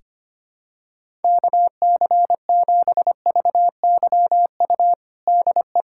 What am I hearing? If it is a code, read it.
Morse code, heard as a keyed tone, about 25 wpm, 700 Hz. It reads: KC7VYU DE
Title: USA Callsign DE Callsign 25wpm